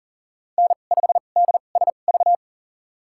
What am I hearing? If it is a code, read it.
Morse code, heard as a keyed tone, about 40 wpm, spaced as for 29 wpm, 700 Hz. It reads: N5DSV
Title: USA Call Signs - 5 Characters - 2x Character Spacing - 40wpm